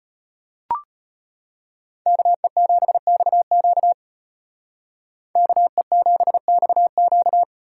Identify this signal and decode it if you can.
Morse code, heard as a keyed tone, about 33 wpm, 700 Hz. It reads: KE7XQ KE7XQ